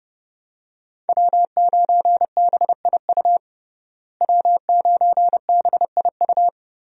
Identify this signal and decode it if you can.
Morse code, heard as a keyed tone, about 30 wpm, 700 Hz. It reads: W9BIU W9BIU
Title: USA Call Signs - 5 Characters - Repeated 3x - 30wpm